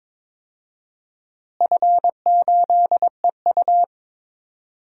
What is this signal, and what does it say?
Morse code, heard as a keyed tone, about 22 wpm, 700 Hz. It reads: F8EU
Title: DXCC-100 Call Signs - 4 Characters - Repeated 2x - 22wpm